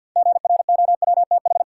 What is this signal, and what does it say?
Morse code, heard as a keyed tone, about 51 wpm, 700 Hz. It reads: GROWTH
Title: Six Letter Words - 50wpm